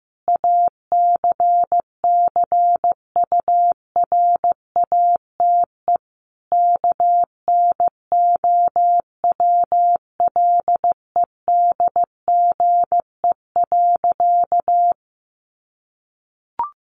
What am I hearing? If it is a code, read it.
Morse code, heard as a keyed tone, about 15 wpm, 700 Hz. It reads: ACCURATE KNOWLEDGE.
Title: Sentences from Top 1200 Words 15wpm